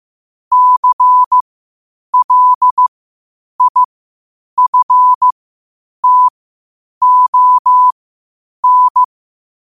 Morse code, heard as a keyed tone, about 15 wpm, spaced as for 8 wpm, 1000 Hz. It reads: CLIFTON
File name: qotdfiles / 8wpm